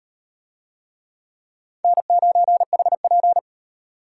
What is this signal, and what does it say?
Morse code, heard as a keyed tone, about 36 wpm, 700 Hz. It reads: N9HP